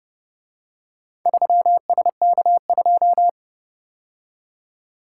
Morse code, heard as a keyed tone, about 31 wpm, 700 Hz. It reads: 3SK2